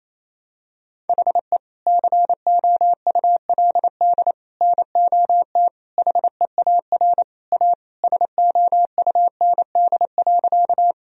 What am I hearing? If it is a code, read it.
Morse code, heard as a keyed tone, about 28 wpm, 700 Hz. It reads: HE COULD NOT HEAR A SOUND.
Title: Sentences from Top 1000 Words - Review Part 1 - 28wpm